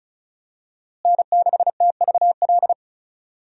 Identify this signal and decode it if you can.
Morse code, heard as a keyed tone, about 34 wpm, 700 Hz. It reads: N6TVL
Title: USA Call Signs - 5 Characters - Repeated 2x - 35wpm